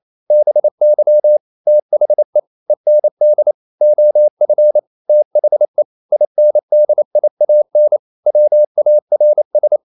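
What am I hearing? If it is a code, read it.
Morse code, heard as a keyed tone, about 28 wpm, 600 Hz. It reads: BY THE END OF THE INDIAN WARS
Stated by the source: Morse code practice at 28 wpm, text of Ken Follett's novel Never in English, from CW med Gnister.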